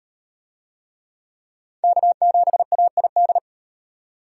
Morse code, heard as a keyed tone, about 35 wpm, 700 Hz. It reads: K7AID